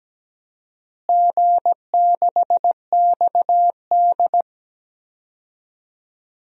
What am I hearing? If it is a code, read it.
Morse code, heard as a keyed tone, about 17 wpm, 700 Hz. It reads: G6XD
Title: DXCC-100 Call Signs - 4 Characters - Repeated 3x - 17wpm